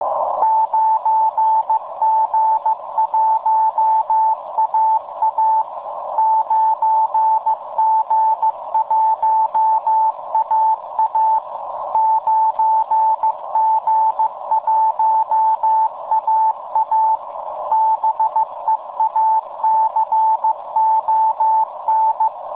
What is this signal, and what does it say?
Morse code, heard as a keyed tone, about 15 wpm, 880 Hz. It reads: 9G1AA 9G1AA 9G1AA BEACON